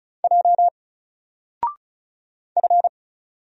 Morse code, heard as a keyed tone, about 35 wpm, 700 Hz. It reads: J F
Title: Single Character (Letter:Number:Punctuation:Prosign) 35wpm